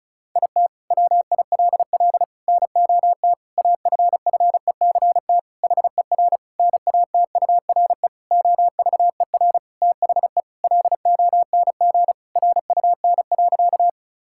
Morse code, heard as a keyed tone, about 34 wpm, 700 Hz. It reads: IT WILL NOT AFFECT HER NATURE OVER THE LONG RUN.
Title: Sentences from Top 1000 Words - Review Part 6 - 35wpm